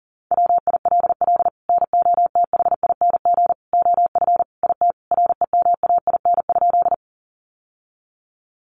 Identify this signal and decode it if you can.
Morse code, heard as a keyed tone, about 40 wpm, 700 Hz. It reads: WILL NOTHING OF IT REMAIN?
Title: Sentences from Top 1200 Words 40wpm